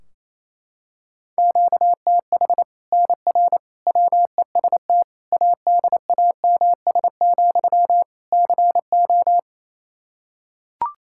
Morse code, heard as a keyed tone, about 28 wpm, 700 Hz. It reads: QTH NR WEST ADAMS, CO
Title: Practice QSO - Exchange 1 - Beginning - 28wpm